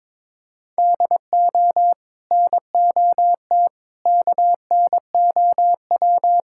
Morse code, heard as a keyed tone, about 22 wpm, 700 Hz. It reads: DO NOT KNOW
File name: practice / 22wpm / Build A Phrase - 5-Word Phrases - 22wpm